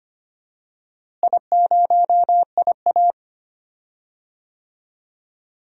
Morse code, heard as a keyed tone, about 25 wpm, 700 Hz. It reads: I0IA
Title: DXCC-100 Call Signs - 4 Characters - Repeated 2x - 25wpm